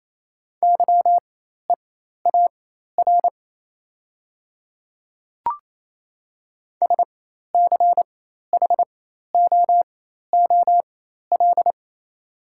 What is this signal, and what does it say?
Morse code, heard as a keyed tone, about 28 wpm, spaced as for 13 wpm, 690 Hz. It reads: YEAR SCHOOL
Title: 4x Character Spacing - Top 300 Words - 28wpm